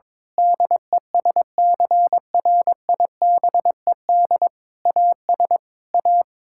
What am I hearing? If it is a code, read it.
Morse code, heard as a keyed tone, about 22 wpm, 700 Hz. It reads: DESCRIBED AS A
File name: practice / 22wpm / Sentences with 8-Word Phrases - 22wpm